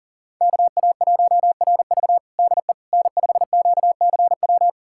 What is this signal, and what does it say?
Morse code, heard as a keyed tone, about 40 wpm, 700 Hz. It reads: KA1RV DE N5QCW
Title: USA Callsign DE Callsign - 5 Characters - 40wpm